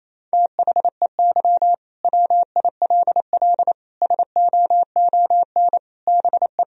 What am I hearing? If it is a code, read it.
Morse code, heard as a keyed tone, about 28 wpm, 700 Hz. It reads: THEY WILL SOON BE
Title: Sentences from Top 1000 Words - Review Part 4 - 28wpm